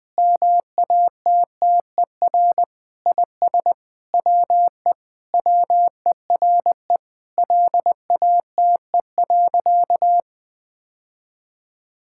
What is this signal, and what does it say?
Morse code, heard as a keyed tone, about 21 wpm, 700 Hz. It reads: MATTER IS WE WERE LATE.
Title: Sentences with 5-Word Phrases - 20wpm